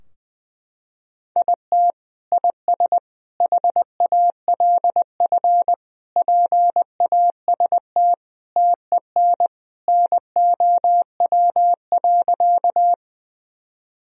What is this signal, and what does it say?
Morse code, heard as a keyed tone, about 20 wpm, 700 Hz. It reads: IT IS HALF PAST TEN NOW.
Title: Sentences from Top 1000 Words - Review Part 3 - 20wpm